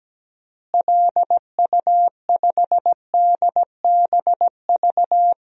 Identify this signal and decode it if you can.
Morse code, heard as a keyed tone, about 17 wpm, 700 Hz. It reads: LU5DBV
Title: DXCC-100 Call Signs - 6 Characters - Repeated 3x - 17wpm